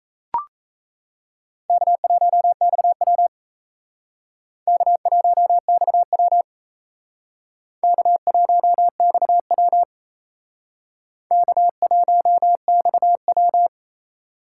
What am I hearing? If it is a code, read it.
Morse code, heard as a keyed tone, about 36 wpm, 700 Hz. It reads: K1XW K1XW K1XW K1XW